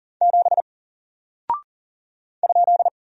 Morse code, heard as a keyed tone, about 40 wpm, 700 Hz. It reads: 7 ?
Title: Single Character (Letter:Number:Punctuation:Prosign) 40wpm